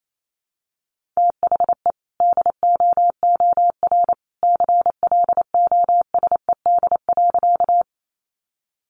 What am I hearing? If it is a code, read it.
Morse code, heard as a keyed tone, about 28 wpm, 700 Hz. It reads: THE DOOR CLOSED.
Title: Sentences from Top 1400 Words 28wpm